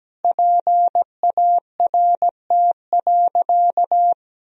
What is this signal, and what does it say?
Morse code, heard as a keyed tone, about 17 wpm, 700 Hz. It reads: PART.
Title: Sentences from Top 1000 Words - Review Part 3 - 17wpm